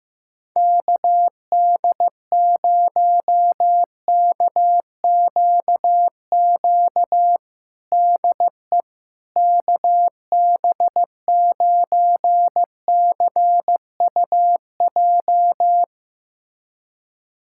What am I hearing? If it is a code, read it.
Morse code, heard as a keyed tone, about 15 wpm, 700 Hz. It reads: KD0KQQ DE KB9CUJ
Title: USA Callsign DE Callsign - 6 Characters - 15wpm